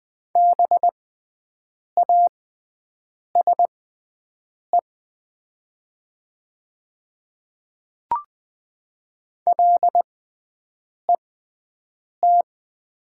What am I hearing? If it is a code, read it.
Morse code, heard as a keyed tone, about 20 wpm, spaced as for 7 wpm, 700 Hz. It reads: BASE LET
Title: 6x Character Spacing - Top 500 Words - 20wpm